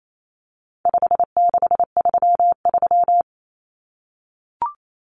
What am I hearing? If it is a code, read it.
Morse code, heard as a keyed tone, about 28 wpm, 700 Hz. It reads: 5633
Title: Numbers - 4-Digit 28wpm